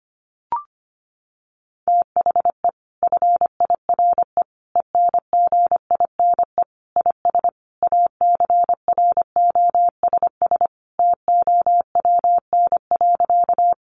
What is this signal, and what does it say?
Morse code, heard as a keyed tone, about 25 wpm, 700 Hz. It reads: THE FIRE ENGINE IS ACROSS TOWN.
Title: Sentences from Top 1500 Words 25wpm